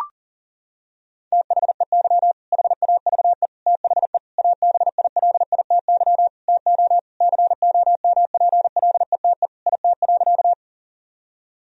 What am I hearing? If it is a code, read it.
Morse code, heard as a keyed tone, about 40 wpm, 700 Hz. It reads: THEY HAVE THE ABILITY TO COMPLETE IT.
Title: Sentences from Top 1000 Words - Review Part 3 - 40wpm